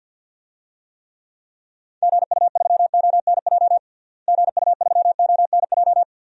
Morse code, heard as a keyed tone, about 52 wpm, 700 Hz. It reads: GU3ONJ GU3ONJ
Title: DXCC-100 Call Signs - 6 Characters - Repeated 2x - 50wpm